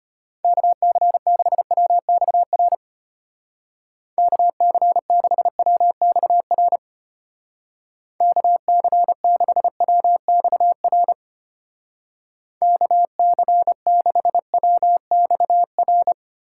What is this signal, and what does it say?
Morse code, heard as a keyed tone, about 32 wpm, 700 Hz. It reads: KC6WXR KC6WXR KC6WXR KC6WXR